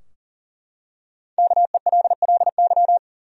Morse code, heard as a keyed tone, about 37 wpm, 700 Hz. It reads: KELLY
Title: USA Common Names - Max 5 Letters - 40wpm